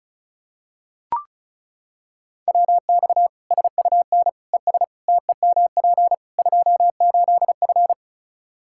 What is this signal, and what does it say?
Morse code, heard as a keyed tone, about 36 wpm, 700 Hz. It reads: WX SUN ES TEMP 28F
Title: Practice QSO - Exchange 2 - Info - 35wpm